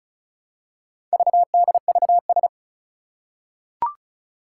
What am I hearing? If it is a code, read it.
Morse code, heard as a keyed tone, about 36 wpm, 700 Hz. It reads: VDVS
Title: Four Challenging Letter-Number Combinations 35wpm